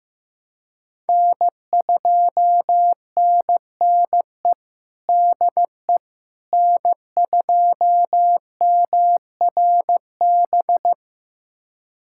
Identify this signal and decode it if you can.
Morse code, heard as a keyed tone, about 15 wpm, 700 Hz. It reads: N2NNE DE N2MRB
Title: USA Callsign DE Callsign - 5 Characters - 15wpm